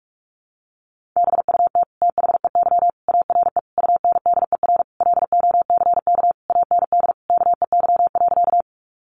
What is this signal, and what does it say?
Morse code, heard as a keyed tone, about 44 wpm, 700 Hz. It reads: BUT THEY ARE UNDER LOCK AND KEY.
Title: Sentences from Top 1300 Words 45wpm